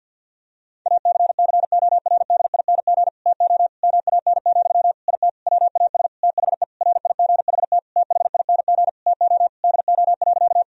Morse code, heard as a keyed tone, about 50 wpm, 700 Hz. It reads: ACCORDING TO MAN, IT WAS THE RIGHT THING TO DO.
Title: Sentences from Top 1000 Words - Review Part 4 - 50wpm